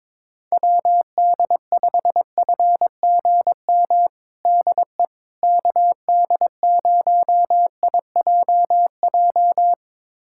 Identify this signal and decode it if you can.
Morse code, heard as a keyed tone, about 22 wpm, 700 Hz. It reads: WD5FGM DE KD0IJJ
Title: USA Callsign DE Callsign - 6 Characters - 22wpm